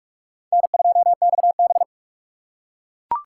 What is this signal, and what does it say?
Morse code, heard as a keyed tone, about 42 wpm, 700 Hz. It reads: N2XB
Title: USA Call Signs - 3x Character Spacing - 45wpm